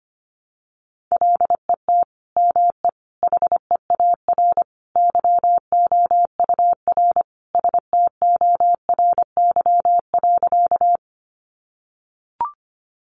Morse code, heard as a keyed tone, about 25 wpm, 700 Hz. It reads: LET ME HEAR YOUR STORY.
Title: Sentences from Top 300 Words 25wpm